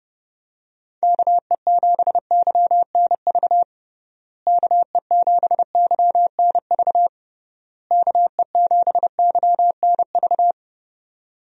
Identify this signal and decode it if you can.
Morse code, heard as a keyed tone, about 32 wpm, 700 Hz. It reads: KE7YNV KE7YNV KE7YNV
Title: USA Call Signs - 6 Characters - Repeated 3x - 30wpm